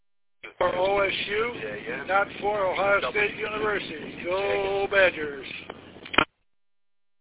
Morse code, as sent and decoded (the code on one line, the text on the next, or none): none